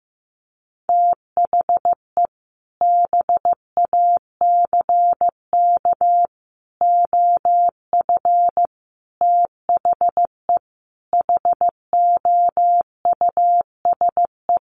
Morse code, heard as a keyed tone, about 15 wpm, 700 Hz. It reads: THE BACK OF THE HOUSE
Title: Sets of 5 Words Encore2 15wpm